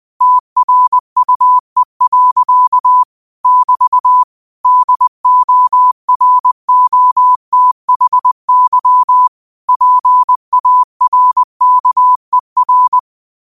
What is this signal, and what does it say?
Morse code, heard as a keyed tone, about 20 wpm, 1000 Hz. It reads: TRUE. <BT> DOROTHY PARKER
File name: 20wpm